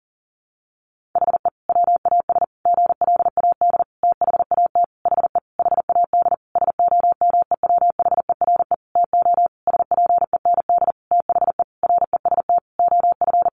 Morse code, heard as a keyed tone, about 40 wpm, 700 Hz. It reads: HE WAS GLAD THAT HE HAD SOMEWHERE TO SPEND THE REST OF
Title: Sentences from Top 1400 Words 40wpm